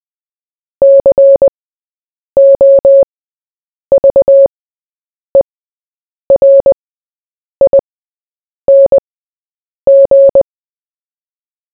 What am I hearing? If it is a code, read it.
Morse code, heard as a keyed tone, about 20 wpm, spaced as for 8 wpm, 550 Hz. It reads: COVERING
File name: Short-Story-4-8wpm